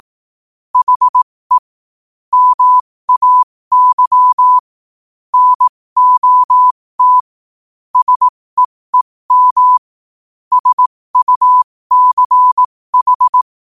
Morse code, heard as a keyed tone, about 18 wpm, spaced as for 15 wpm, 1000 Hz. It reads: HE MAY NOT SEEM SUCH
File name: qotdfiles / 15wpm